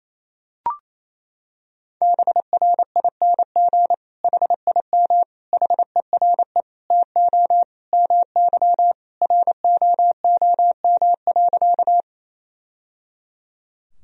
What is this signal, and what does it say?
Morse code, heard as a keyed tone, about 28 wpm, 700 Hz. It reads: BRING HIM HERE TO MY ROOM.
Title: Sentences from Top 1000 Words - Review Part 3 - 28wpm